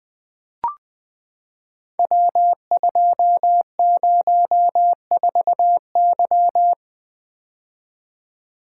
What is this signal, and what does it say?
Morse code, heard as a keyed tone, about 20 wpm, 700 Hz. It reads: W204Y
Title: Code Groups - 5 Letter:Numbers - 20wpm